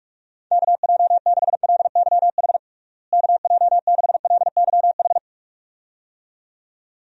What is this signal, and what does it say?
Morse code, heard as a keyed tone, about 41 wpm, 700 Hz. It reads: KJ6LYH KJ6LYH
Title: USA Call Signs - 6 Characters - Repeated 2x - 45wpm